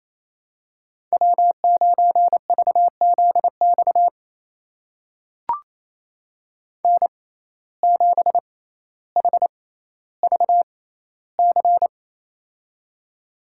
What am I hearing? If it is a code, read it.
Morse code, heard as a keyed tone, about 28 wpm, spaced as for 14 wpm, 700 Hz. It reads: W9VZX N7HVC
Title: USA Call Signs - 5 Characters - 6x Character Spacing - 28wpm